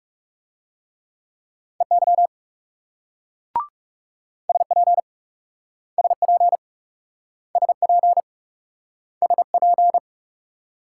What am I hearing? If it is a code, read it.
Morse code, heard as a keyed tone, about 40 wpm, 700 Hz. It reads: EY SP SP SP SP